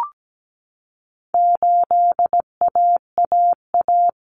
Morse code, heard as a keyed tone, about 17 wpm, 700 Hz. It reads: 8AAA
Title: Common Cut Numbers - 4-Digit 17wpm